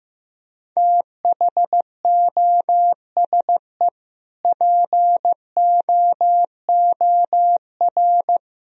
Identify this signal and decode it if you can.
Morse code, heard as a keyed tone, about 15 wpm, 700 Hz. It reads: THOSE POOR
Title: Sentences from Top 1000 Words - Review Part 1 - 15wpm